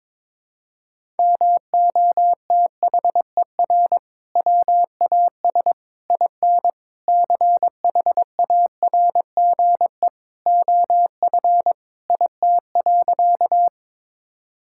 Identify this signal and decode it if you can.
Morse code, heard as a keyed tone, about 22 wpm, 700 Hz. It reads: MOTHER WAS IN CHARGE OF IT.
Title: Sentences from Top 1000 Words - Review Part 6 - 22wpm